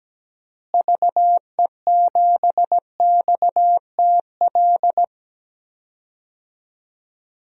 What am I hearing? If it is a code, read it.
Morse code, heard as a keyed tone, about 17 wpm, 700 Hz. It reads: VE7XTL
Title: DXCC-100 Call Signs - 6 Characters - Repeated 3x - 17wpm